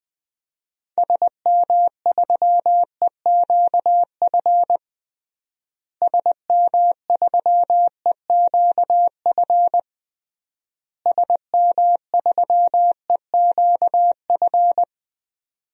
Morse code, heard as a keyed tone, about 21 wpm, 700 Hz. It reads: SM3EQF SM3EQF SM3EQF
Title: DXCC-100 Call Signs - 6 Characters - Repeated 3x - 20wpm